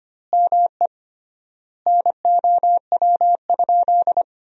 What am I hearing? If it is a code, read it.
Morse code, heard as a keyed tone, about 25 wpm, 700 Hz. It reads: ME NOW?
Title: 3x Word-Spacing - Sentences from Top 400 Words - 25wpm